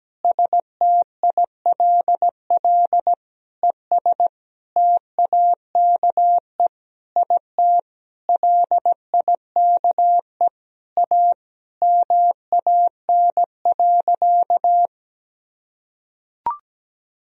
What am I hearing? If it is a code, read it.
Morse code, heard as a keyed tone, about 17 wpm, 700 Hz. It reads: STILL ES TAKE IT LIKE A MAN.
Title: Sentences from Top 300 Words and Abbreviations 17wpm